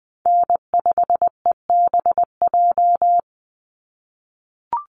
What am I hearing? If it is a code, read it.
Morse code, heard as a keyed tone, about 20 wpm, 700 Hz. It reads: N5EBJ
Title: USA Call Signs 20wpm